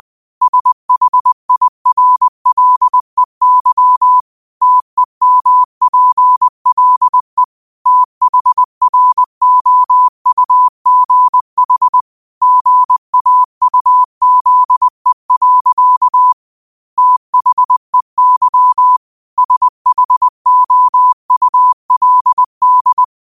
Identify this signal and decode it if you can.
Morse code, heard as a keyed tone, about 20 wpm, 1000 Hz. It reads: SHIRLEY TEMPLE THROUGH GAUZE. THEY SHOULD
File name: qotdfiles / 20wpm